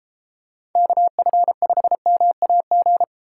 Morse code, heard as a keyed tone, about 31 wpm, 700 Hz. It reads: KF5MAG